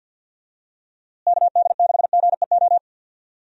Morse code, heard as a keyed tone, about 52 wpm, 700 Hz. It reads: KD6GEO
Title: USA Call Signs - 6 Characters - Repeated 2x - 50wpm